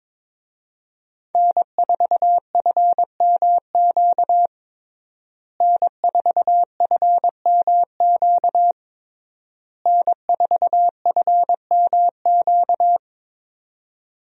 Morse code, heard as a keyed tone, about 22 wpm, 700 Hz. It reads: N4FMQ N4FMQ N4FMQ
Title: USA Call Signs - 5 Characters - Repeated 3x - 22wpm